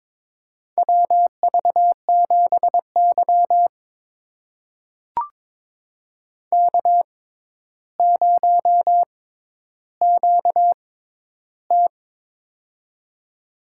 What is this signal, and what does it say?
Morse code, heard as a keyed tone, about 22 wpm, spaced as for 11 wpm, 700 Hz. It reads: WV7Y K0QT
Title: USA Call Signs - 4 Characters - 6x Character Spacing - 22wpm